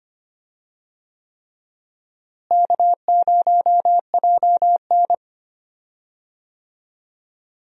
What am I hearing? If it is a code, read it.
Morse code, heard as a keyed tone, about 25 wpm, 700 Hz. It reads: K0JN
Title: USA Call Signs - 4 Characters - Repeated 3x - 25wpm